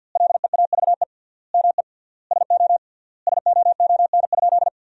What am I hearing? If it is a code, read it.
Morse code, heard as a keyed tone, about 50 wpm, 700 Hz. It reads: LEAVE ME SO SOON?
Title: 3x Word-Spacing - Sentences from Top 400 Words - 50wpm